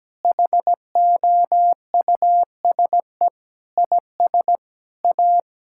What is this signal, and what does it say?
Morse code, heard as a keyed tone, about 17 wpm, 700 Hz. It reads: HOUSE IS A
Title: Sentences from Top 1600 Words 17wpm